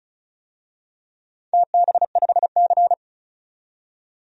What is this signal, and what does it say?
Morse code, heard as a keyed tone, about 36 wpm, 700 Hz. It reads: TB5C
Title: DXCC-100 Call Signs - 4 Characters - 35wpm